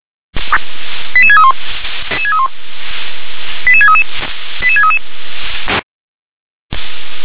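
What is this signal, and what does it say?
Five-tone selective call (ZVEI), heard as quick runs of five tones, tones of about 70 ms. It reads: 88631 88633 88633
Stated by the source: ZVEI five-tone call 88149 received on SDR.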